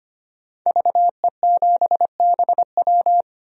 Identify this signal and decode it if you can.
Morse code, heard as a keyed tone, about 25 wpm, 700 Hz. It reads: VE7BW
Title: DXCC-100 Call Signs - 5 Characters - 25wpm